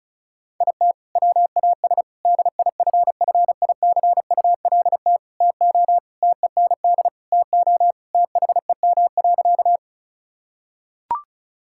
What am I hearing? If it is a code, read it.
Morse code, heard as a keyed tone, about 34 wpm, 700 Hz. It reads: IT WAS DIFFICULT TO TEND TO THEM.
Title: Sentences from Top 1000 Words - Review Part 5 - 35wpm